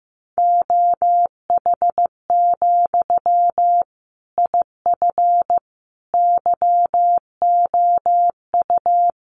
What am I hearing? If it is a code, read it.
Morse code, heard as a keyed tone, about 15 wpm, 700 Hz. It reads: OH, IF YOU
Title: Sentences from Top 1100 Words 15wpm